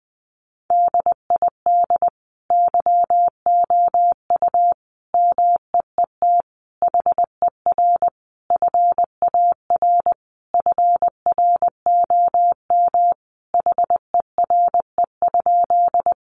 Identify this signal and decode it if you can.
Morse code, heard as a keyed tone, about 20 wpm, 700 Hz. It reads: DID YOU MEET HER FAR FROM HERE?
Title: Sentences from Top 1100 Words 20wpm